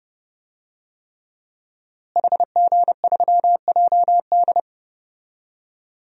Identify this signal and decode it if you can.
Morse code, heard as a keyed tone, about 32 wpm, 700 Hz. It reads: HG3JD
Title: DXCC-100 Call Signs - 5 Characters - Repeated 3x - 30wpm